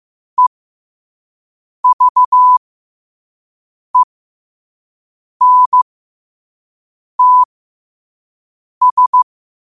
Morse code, heard as a keyed tone, about 15 wpm, spaced as for 5 wpm, 1000 Hz. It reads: EVENTS
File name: qotdfiles / 5wpm